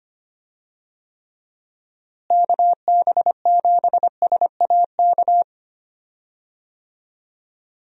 Morse code, heard as a keyed tone, about 25 wpm, 700 Hz. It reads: KB7SAK
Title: USA Call Signs - 6 Characters - Repeated 2x - 25wpm